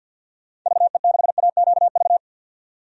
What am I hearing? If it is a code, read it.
Morse code, heard as a keyed tone, about 47 wpm, 700 Hz. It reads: VE6AQV
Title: DXCC-100 Call Signs - 6 Characters - 50wpm